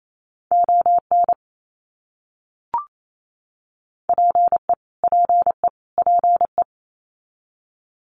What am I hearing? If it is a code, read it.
Morse code, heard as a keyed tone, about 28 wpm, 700 Hz. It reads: ON PE PE PE
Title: Canadian Provinces and Territories - Repeated 3x - 28wpm